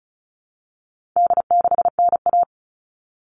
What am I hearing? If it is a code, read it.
Morse code, heard as a keyed tone, about 34 wpm, 700 Hz. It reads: D6NA
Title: All Cut Numbers - 4-Digit 35wpm